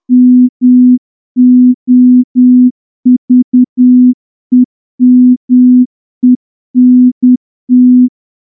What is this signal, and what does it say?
Morse code, heard as a keyed tone, about 10 wpm, 250 Hz. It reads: MOVEMENT
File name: morse-training-10WPM